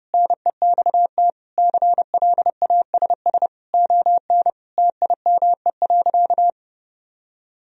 Morse code, heard as a keyed tone, about 30 wpm, 700 Hz. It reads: NEXT CLASS ON TIME.
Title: Sentences from Top 1000 Words - Review Part 2 - 30wpm